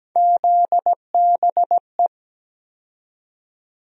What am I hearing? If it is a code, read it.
Morse code, heard as a keyed tone, about 17 wpm, 700 Hz. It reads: ZBE
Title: USA Call Sign Postfixes - 3 Letters 17wpm